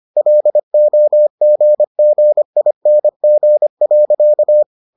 Morse code, heard as a keyed tone, about 25 wpm, 600 Hz. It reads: LOGGING.